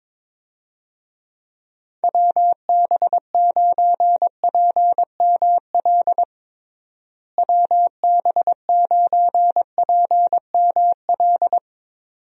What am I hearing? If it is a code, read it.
Morse code, heard as a keyed tone, about 22 wpm, 700 Hz. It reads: WB9PML WB9PML
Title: USA Call Signs - 6 Characters - Repeated 2x - 22wpm